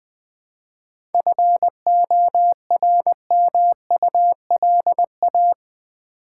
Morse code, heard as a keyed tone, about 21 wpm, 700 Hz. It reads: FORMULA
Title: 7 Letter Words - First 3 Letters Given - 20wpm